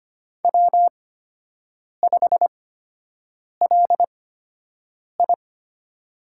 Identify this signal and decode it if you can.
Morse code, heard as a keyed tone, about 25 wpm, spaced as for 7 wpm, 700 Hz. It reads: W5LI
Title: USA Call Signs - 4 Characters - 8x Character Spacing - 25wpm